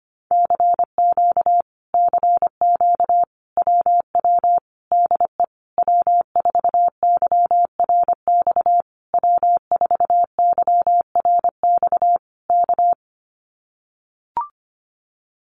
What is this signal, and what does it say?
Morse code, heard as a keyed tone, about 25 wpm, 700 Hz. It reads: CQ CQ WW DE W4YRX W4YRX K
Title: Common QSO Phrases Encore 25wpm